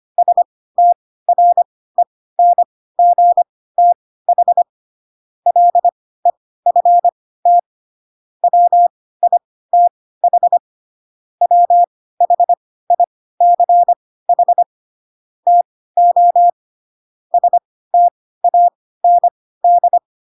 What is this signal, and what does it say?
Morse code, heard as a keyed tone, about 25 wpm, spaced as for 16 wpm, 700 Hz. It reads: STRENGTH LEFT WITH WHICH TO STAND